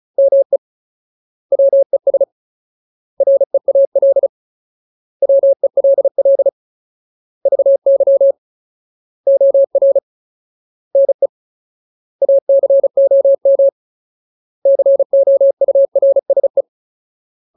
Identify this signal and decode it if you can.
Morse code, heard as a keyed tone, about 34 wpm, 550 Hz. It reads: ME WES REAL WELL VY OR NE ACOM COURSE
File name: Ultimate Word List / 35WPM Ultimate